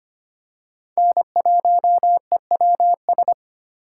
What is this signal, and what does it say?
Morse code, heard as a keyed tone, about 25 wpm, 700 Hz. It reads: N1EWS
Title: USA Call Signs - 5 Characters - 2x Character Spacing - 25wpm